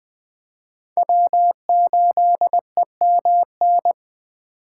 Morse code, heard as a keyed tone, about 20 wpm, 700 Hz. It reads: W8EMN